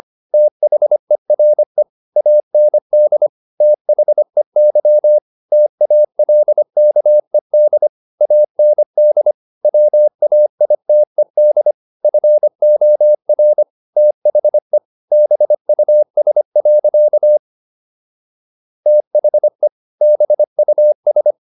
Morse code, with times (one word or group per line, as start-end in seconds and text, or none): 0.3-1.8 THERE
2.2-3.3 AND
3.6-5.2 THEY
5.5-7.9 TALKED
8.2-9.3 AND
9.6-11.7 WAITED
12.0-13.6 FOR
14.0-14.8 THE
15.1-17.4 BUS.
18.9-19.7 THE
20.0-21.3 BUS